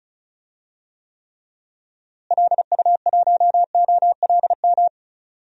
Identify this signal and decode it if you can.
Morse code, heard as a keyed tone, about 36 wpm, 700 Hz. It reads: LU1OLM
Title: DXCC-100 Call Signs - 6 Characters - Repeated 3x - 35wpm